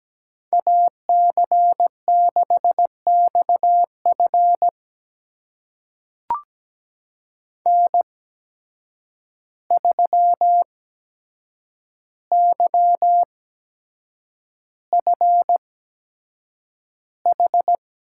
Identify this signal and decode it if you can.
Morse code, heard as a keyed tone, about 17 wpm, spaced as for 7 wpm, 700 Hz. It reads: AC6XF N3YFH
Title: USA Call Signs - 5 Characters - 8x Character Spacing - 17wpm